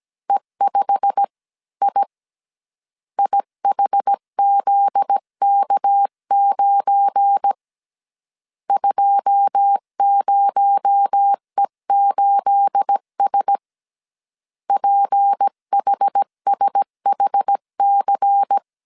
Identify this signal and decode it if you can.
Morse code, heard as a keyed tone, about 17 wpm, 800 Hz. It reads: E5I IHZK9 20E8S PHSHC
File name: CW / DET / 1 9 1500C S B 16wpm 2856